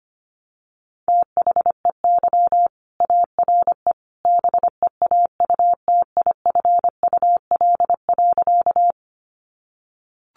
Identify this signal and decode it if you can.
Morse code, heard as a keyed tone, about 25 wpm, 700 Hz. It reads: THEY ARE BEAUTIFUL.
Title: Sentences from Top 900 Words 25wpm